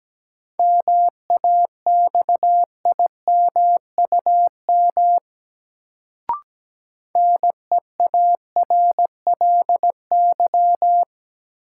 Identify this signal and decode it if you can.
Morse code, heard as a keyed tone, about 17 wpm, 700 Hz. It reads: MAXIMUM NEARLY
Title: Sentences from Top 1300 Words - New Word Drill Alt - 17wpm